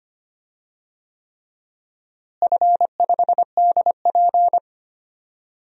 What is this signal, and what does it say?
Morse code, heard as a keyed tone, about 25 wpm, 700 Hz. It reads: F5DP
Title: DXCC-100 Call Signs - 4 Characters - Repeated 3x - 25wpm